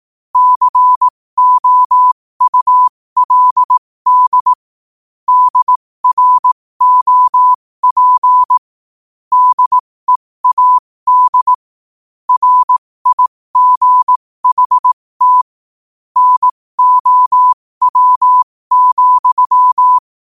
Morse code, heard as a keyed tone, about 18 wpm, spaced as for 15 wpm, 1000 Hz. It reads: COULD DROP DEAD RIGHT NOW,
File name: qotdfiles / 15wpm